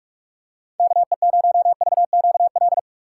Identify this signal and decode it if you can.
Morse code, heard as a keyed tone, about 42 wpm, 700 Hz. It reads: KE0VQL
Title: USA Call Signs - 4x Character Spacing - 45wpm